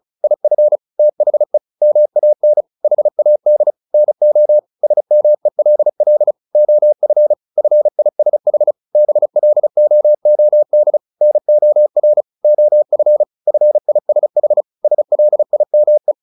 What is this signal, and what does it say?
Morse code, heard as a keyed tone, about 34 wpm, 600 Hz. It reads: IF THE MAN HAD NO SMELL OF FISH BLOOD NOR OF FISH SLIME
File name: the old man and the sea 2 35wpm